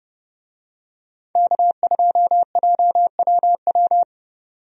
Morse code, heard as a keyed tone, about 30 wpm, 700 Hz. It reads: K2JWW
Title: USA Call Signs - 5 Characters - 30wpm